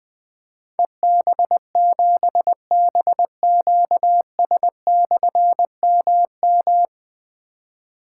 Morse code, heard as a keyed tone, about 20 wpm, 700 Hz. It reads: EB7BQS/MM
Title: Difficult Call Signs 20wpm